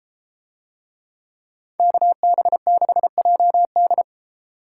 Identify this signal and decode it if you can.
Morse code, heard as a keyed tone, about 31 wpm, 700 Hz. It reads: KB6JD